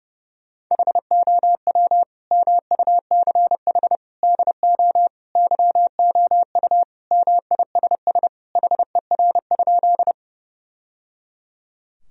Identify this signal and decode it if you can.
Morse code, heard as a keyed tone, about 30 wpm, 700 Hz. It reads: HOW MUCH DO YOU MISS HER?
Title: Sentences from Top 1000 Words - Review Part 3 - 30wpm